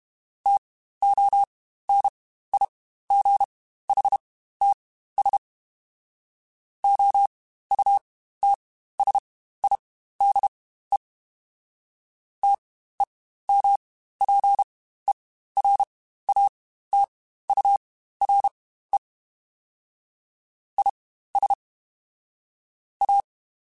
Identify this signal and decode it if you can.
Morse code, heard as a keyed tone, about 32 wpm, spaced as for 15 wpm, 800 Hz. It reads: TONIGHTS OUTSIDE TEMPERATURE IS A
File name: Spaced Characters / Spaced Characters-mp3s / Weather4 10 sp @30